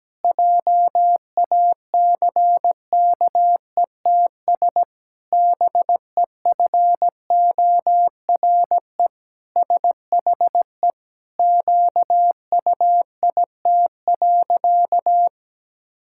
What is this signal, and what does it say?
Morse code, heard as a keyed tone, about 17 wpm, 700 Hz. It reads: JACKETS BEFORE SHE QUIT.
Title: Pangram Challenge v2 - 17wpm